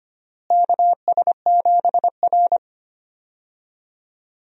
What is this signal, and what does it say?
Morse code, heard as a keyed tone, about 25 wpm, 700 Hz. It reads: KS7R
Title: USA Call Signs - 4 Characters - Repeated 3x - 25wpm